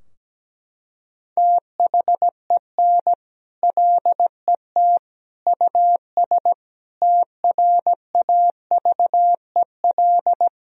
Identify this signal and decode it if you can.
Morse code, heard as a keyed tone, about 17 wpm, 700 Hz. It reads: THEN LET US TRAVEL
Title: Sentences from Top 1000 Words - Review Part 2 - 17wpm